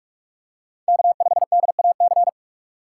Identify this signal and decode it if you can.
Morse code, heard as a keyed tone, about 46 wpm, 700 Hz. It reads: K5DAC
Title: USA Call Signs - 5 Characters - 4x Character Spacing - 45wpm